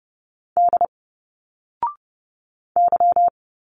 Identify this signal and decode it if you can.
Morse code, heard as a keyed tone, about 30 wpm, 700 Hz. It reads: D Y
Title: Single Letters 30wpm